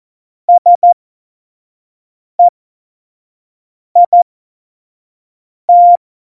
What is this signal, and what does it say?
Morse code, heard as a keyed tone, about 14 wpm, spaced as for 5 wpm, 700 Hz. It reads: SEIT
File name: Morsekurs / Witze / Witz 06. November 2018, 25 F70